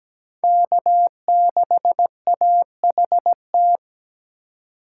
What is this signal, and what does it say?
Morse code, heard as a keyed tone, about 17 wpm, 700 Hz. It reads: K6AHT
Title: USA Call Signs - 5 Characters - 3x Character Spacing - 17wpm